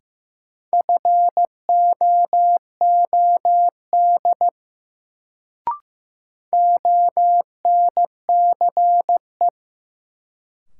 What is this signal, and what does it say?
Morse code, heard as a keyed tone, about 15 wpm, 700 Hz. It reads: FOOD ONCE
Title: Sentences from Top 300 Words - New Word Drill - 15wpm